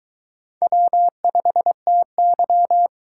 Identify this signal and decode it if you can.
Morse code, heard as a keyed tone, about 23 wpm, 700 Hz. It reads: W5TY